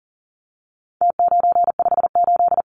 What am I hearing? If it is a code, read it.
Morse code, heard as a keyed tone, about 40 wpm, 700 Hz. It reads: T958
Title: Numbers - Contest Style - 4-Digit 40wpm